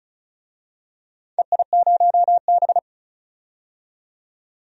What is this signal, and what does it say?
Morse code, heard as a keyed tone, about 36 wpm, 700 Hz. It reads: EI0B
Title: DXCC-100 Call Signs - 4 Characters - 35wpm